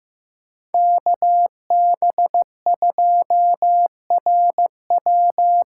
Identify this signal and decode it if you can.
Morse code, heard as a keyed tone, about 15 wpm, 700 Hz. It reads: KB2RW
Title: USA Call Signs - 6 Characters - 6x Character Spacing - 15wpm